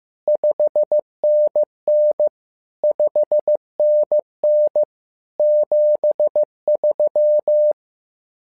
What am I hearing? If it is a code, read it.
Morse code, heard as a keyed tone, about 15 wpm, 600 Hz. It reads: 5NN 5NN 73
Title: SOTA QSO Practice 15wpm